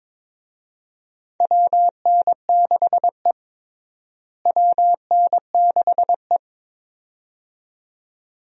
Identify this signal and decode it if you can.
Morse code, heard as a keyed tone, about 22 wpm, 700 Hz. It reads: WN6E WN6E
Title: USA Call Signs - 4 Characters - Repeated 2x - 22wpm